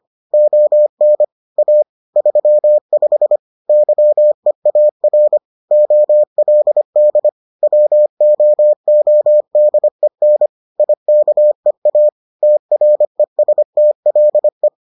Morse code, heard as a keyed tone, about 25 wpm, 600 Hz. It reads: ON A 35 YEAR OLD WOODEN IKEA TRESTLE